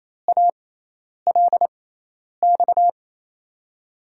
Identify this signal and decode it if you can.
Morse code, heard as a keyed tone, about 28 wpm, spaced as for 10 wpm, 700 Hz. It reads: ALX
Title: USA Call Signs - 5 Characters - 6x Character Spacing - 28wpm